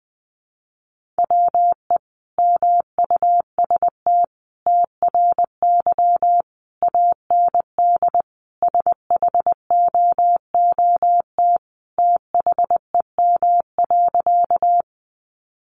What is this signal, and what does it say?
Morse code, heard as a keyed tone, about 19 wpm, 700 Hz. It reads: WE MUST TRY AND SHOOT THEM.
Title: Sentences from Top 1200 Words 20wpm